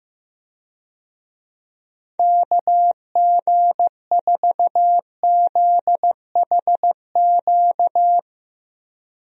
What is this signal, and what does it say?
Morse code, heard as a keyed tone, about 15 wpm, 700 Hz. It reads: KG4ZHQ
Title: USA Call Signs - 6 Characters - Repeated 3x - 15wpm